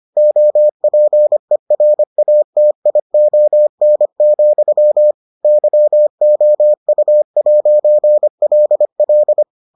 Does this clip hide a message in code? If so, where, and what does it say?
Yes: Morse code from 0.2 to 9.4 s, heard as a keyed tone, about 25 wpm, 600 Hz. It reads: OPERATION, YOU'LL